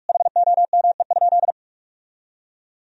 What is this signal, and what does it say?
Morse code, heard as a keyed tone, about 46 wpm, 700 Hz. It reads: HOME?
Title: Sentences from Top 1000 Words - Review Part 4 - 45wpm